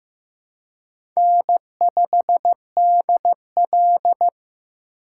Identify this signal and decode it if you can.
Morse code, heard as a keyed tone, about 15 wpm, 700 Hz. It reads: N5DL
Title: USA Call Signs - 3x Character Spacing - 15wpm